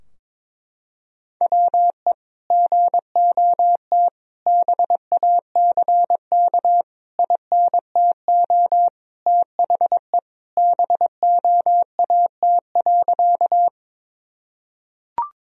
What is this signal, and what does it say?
Morse code, heard as a keyed tone, about 22 wpm, 700 Hz. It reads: WE GOT BACK INTO THE BOAT.
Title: Sentences from Top 1000 Words - Review Part 3 - 22wpm